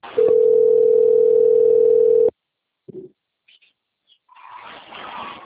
Ringback tone (North America), 440 + 480 Hz.
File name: out- - - - - .8278